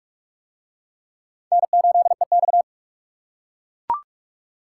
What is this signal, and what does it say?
Morse code, heard as a keyed tone, about 40 wpm, 700 Hz. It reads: N8EX